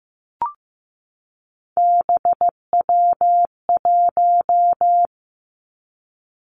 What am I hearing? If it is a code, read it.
Morse code, heard as a keyed tone, about 15 wpm, 700 Hz. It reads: BW1
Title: Three Challenging Letter-Number Combinations 15wpm